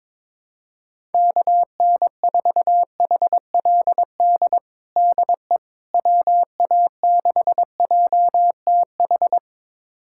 Morse code, heard as a keyed tone, about 22 wpm, 700 Hz. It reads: KN4HLD DE WA6JTH
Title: USA Callsign DE Callsign 22wpm